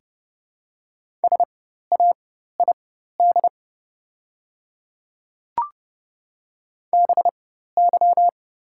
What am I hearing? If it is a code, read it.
Morse code, heard as a keyed tone, about 30 wpm, spaced as for 14 wpm, 700 Hz. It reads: SAID BY
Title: 4x Character Spacing - Top 100 Words - 30wpm